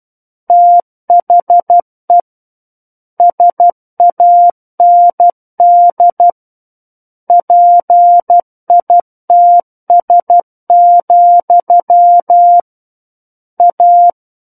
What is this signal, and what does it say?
Morse code, heard as a keyed tone, about 12 wpm, 700 Hz. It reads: THE SAND PITS, A
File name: Book One / 03-Chapter 3 - 12 WPM